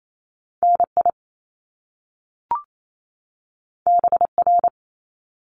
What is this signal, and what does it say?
Morse code, heard as a keyed tone, about 28 wpm, 700 Hz. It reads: NI BR
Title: Top Two Letter Combinations in Words 28wpm